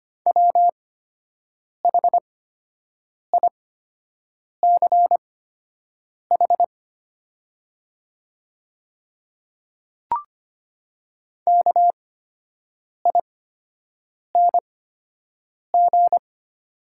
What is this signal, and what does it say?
Morse code, heard as a keyed tone, about 25 wpm, spaced as for 7 wpm, 690 Hz. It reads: WHICH KING
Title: 8x Character Spacing - Top 500 Words - 25wpm